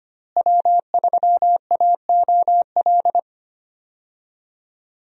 Morse code, heard as a keyed tone, about 25 wpm, 700 Hz. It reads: W3AOL
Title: USA Call Signs - 5 Characters - Repeated 2x - 25wpm